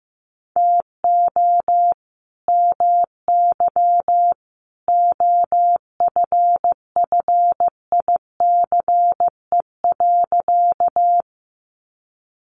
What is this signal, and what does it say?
Morse code, heard as a keyed tone, about 15 wpm, 700 Hz. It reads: TO MY OFFICE.
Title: Sentences from Top 1000 Words 15wpm